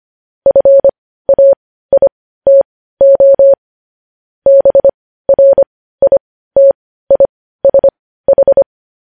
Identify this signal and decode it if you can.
Morse code, heard as a keyed tone, about 25 wpm, spaced as for 15 wpm, 550 Hz. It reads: FAITO BRITISH
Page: news-headlines-in-morse-code-at-15-wpm